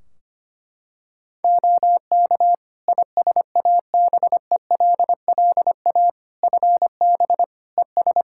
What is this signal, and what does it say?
Morse code, heard as a keyed tone, about 25 wpm, 700 Hz. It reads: OK ISABELLA FB ES
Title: Practice QSO - Exchange 2 - Info - 25wpm